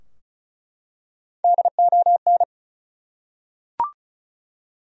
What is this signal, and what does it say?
Morse code, heard as a keyed tone, about 36 wpm, 700 Hz. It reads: DON
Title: USA Common Names - 35wpm